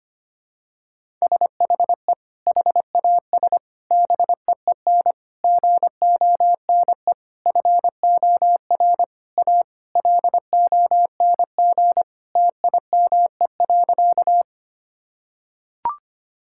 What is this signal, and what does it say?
Morse code, heard as a keyed tone, about 25 wpm, 700 Hz. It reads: SHE HAS BEEN GONE FOR A LONG TIME.
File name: practice / 25wpm / Sentences with 3-Word Phrases - 25wpm